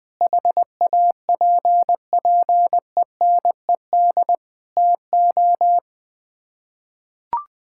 Morse code, heard as a keyed tone, about 20 wpm, 700 Hz. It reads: HAPPENED TO
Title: Sets of 8 Words - 20wpm